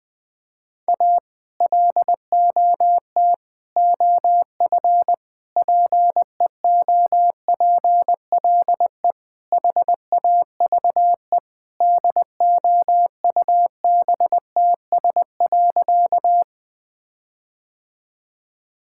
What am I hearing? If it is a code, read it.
Morse code, heard as a keyed tone, about 20 wpm, 700 Hz. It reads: A LOT OF PEOPLE HAVE DOUBTS.
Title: Sentences with 5-Word Phrases - Spoken Beginning - 20wpm